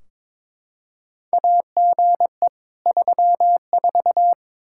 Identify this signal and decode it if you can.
Morse code, heard as a keyed tone, about 22 wpm, 700 Hz. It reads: AGE 34
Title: Practice QSO - Exchange 3 - More Info - 22wpm